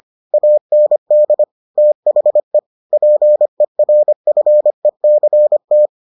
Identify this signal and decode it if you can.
Morse code, heard as a keyed tone, about 25 wpm, 600 Hz. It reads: AND THE PERFECT